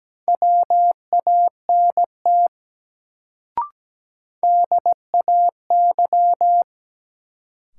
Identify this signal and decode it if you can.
Morse code, heard as a keyed tone, about 17 wpm, 700 Hz. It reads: WANT DAY
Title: Sentences from Top 200 Words - New Word Drill - 17wpm